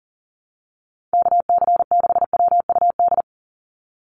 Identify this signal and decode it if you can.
Morse code, heard as a keyed tone, about 40 wpm, 690 Hz. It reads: KC6WUD
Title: USA Call Signs 40wpm